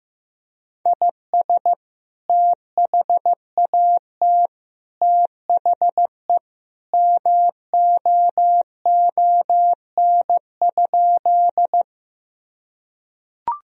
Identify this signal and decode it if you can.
Morse code, heard as a keyed tone, about 15 wpm, 700 Hz. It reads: IS THAT THE MOON?
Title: Sentences from Top 1000 Words - Review Part 3 - 15wpm